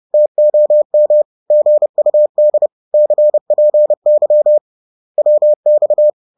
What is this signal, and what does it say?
Morse code, heard as a keyed tone, about 30 wpm, 600 Hz. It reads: TOM GUD CPY WX